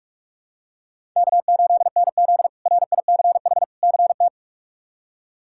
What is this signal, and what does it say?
Morse code, heard as a keyed tone, about 46 wpm, 700 Hz. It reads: K8NZ RICH CT
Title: K1USN SST QSO Practice - 45wpm